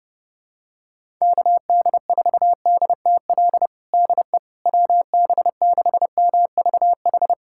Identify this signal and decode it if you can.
Morse code, heard as a keyed tone, about 32 wpm, 700 Hz. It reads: KD4DTL DE WB6MVH
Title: USA Callsign DE Callsign - 6 Characters - 30wpm